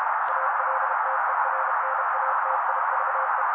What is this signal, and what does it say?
Morse code, heard as a keyed tone, about 32 wpm, 530 Hz. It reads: ALTERNATIVE
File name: daily / 31wpm noise